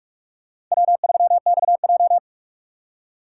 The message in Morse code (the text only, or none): W3XJ